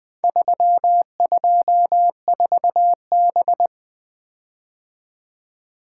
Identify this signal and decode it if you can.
Morse code, heard as a keyed tone, about 20 wpm, 700 Hz. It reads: 324B
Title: Four Letter-Number Combinations - 20wpm